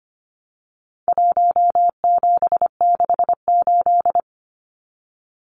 Morse code, heard as a keyed tone, about 25 wpm, 700 Hz. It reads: 1768
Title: Numbers - Contest Style - 4-Digit 25wpm